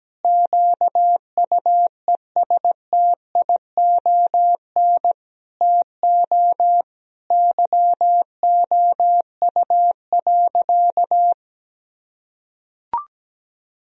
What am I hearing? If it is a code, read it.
Morse code, heard as a keyed tone, about 17 wpm, 700 Hz. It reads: QUESTION TO YOU.
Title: Sentences from Top 1000 Words - Review Part 2 - 17wpm